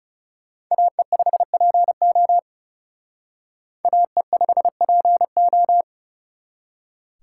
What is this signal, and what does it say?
Morse code, heard as a keyed tone, about 37 wpm, 700 Hz. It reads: AE5PO AE5PO
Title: Speed Racing - USA 2x2 Call Signs 30wpm